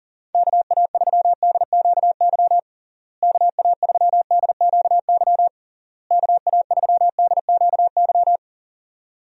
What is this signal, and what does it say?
Morse code, heard as a keyed tone, about 40 wpm, 700 Hz. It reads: KA3DQY KA3DQY KA3DQY
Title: USA Call Signs - 6 Characters - Repeated 3x - 40wpm